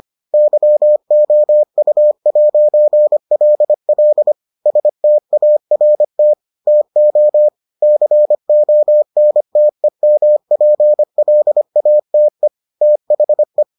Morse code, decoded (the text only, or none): YOU'LL START TO CONTEMPLATE THE